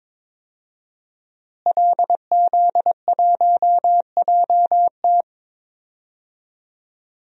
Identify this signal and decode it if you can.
Morse code, heard as a keyed tone, about 22 wpm, 700 Hz. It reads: LZ1JT
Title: DXCC-100 Call Signs - 5 Characters - Repeated 3x - 22wpm